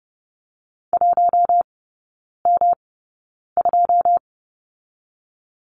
Morse code, heard as a keyed tone, about 30 wpm, spaced as for 9 wpm, 700 Hz. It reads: 1M2
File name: Lesson 18e - Letter m - Three Characters 30wpm